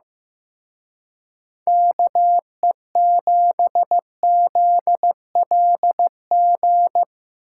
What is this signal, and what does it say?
Morse code, heard as a keyed tone, about 17 wpm, 700 Hz. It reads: KE7ZLG
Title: Speed Racing - USA Call Signs 15wpm